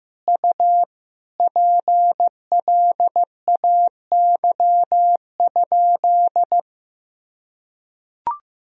Morse code, heard as a keyed tone, about 15 wpm, 700 Hz. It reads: U PLAY?
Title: Sentences from Top 200 Words and Abbreviations 15wpm